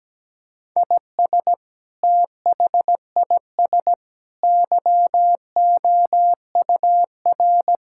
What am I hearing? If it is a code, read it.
Morse code, heard as a keyed tone, about 17 wpm, 700 Hz. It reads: IS THIS YOUR
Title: Sentences from Top 1000 Words - Review Part 6 - 17wpm